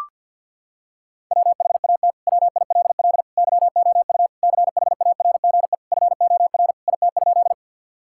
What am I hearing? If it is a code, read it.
Morse code, heard as a keyed tone, about 50 wpm, 700 Hz. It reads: WHAT WILL YOU CHARGE FOR IT?
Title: Sentences from Top 1000 Words - Review Part 6 - 50wpm